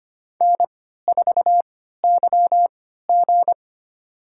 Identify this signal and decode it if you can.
Morse code, heard as a keyed tone, about 25 wpm, spaced as for 14 wpm, 700 Hz. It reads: N4YG
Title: USA Call Signs - 3x Character Spacing - 25wpm